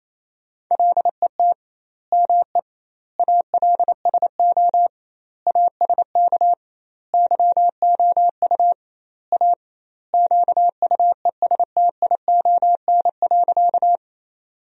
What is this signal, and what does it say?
Morse code, heard as a keyed tone, about 28 wpm, 700 Hz. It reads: LET ME ALSO ASK YOU A QUESTION.
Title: 2x Word-Spacing - Sentences from Top 400 Words - 28wpm